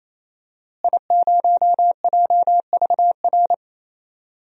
Morse code, heard as a keyed tone, about 28 wpm, 700 Hz. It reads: I0JVR
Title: DXCC-100 Call Signs - 5 Characters - Repeated 3x - 28wpm